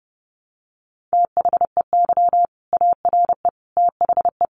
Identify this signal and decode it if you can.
Morse code, heard as a keyed tone, about 30 wpm, 700 Hz. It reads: THEY ARE THE
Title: Sentences from Top 1200 Words 30wpm